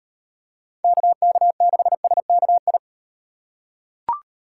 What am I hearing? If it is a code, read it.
Morse code, heard as a keyed tone, about 38 wpm, 700 Hz. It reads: KK6SKI